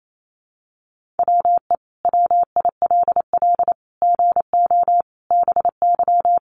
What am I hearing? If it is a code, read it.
Morse code, heard as a keyed tone, about 28 wpm, 700 Hz. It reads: WE WILL GO BY
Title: Sentences from Top 1500 Words 28wpm